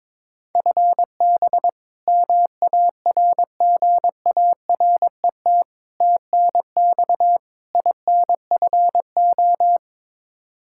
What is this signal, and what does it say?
Morse code, heard as a keyed tone, about 22 wpm, 700 Hz. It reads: FB MARGARET TNX INFO
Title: Practice QSO - Exchange 3 - More Info - 22wpm